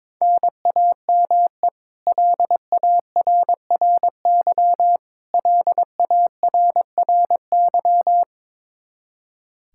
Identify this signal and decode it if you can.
Morse code, heard as a keyed tone, about 22 wpm, 700 Hz. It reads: NAME LARRY LARRY
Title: Practice QSO - Exchange 1 - Beginning - 22wpm